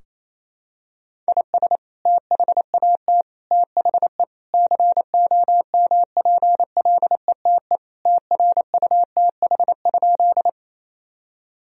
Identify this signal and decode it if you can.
Morse code, heard as a keyed tone, about 28 wpm, 700 Hz. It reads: IS THAT THE COMPLETE TRUTH?
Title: Sentences from Top 1000 Words - Review Part 2 - 28wpm